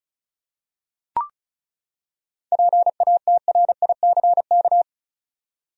Morse code, heard as a keyed tone, about 36 wpm, 700 Hz. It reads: PATRICK
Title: USA Common Names - 35wpm